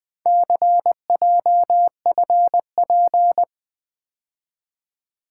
Morse code, heard as a keyed tone, about 20 wpm, 700 Hz. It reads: CJFP
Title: Four Letter-Number Combinations - 20wpm